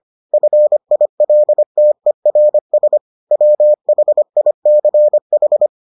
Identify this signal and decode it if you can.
Morse code, heard as a keyed tone, about 25 wpm, 600 Hz. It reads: FILTERS WHICH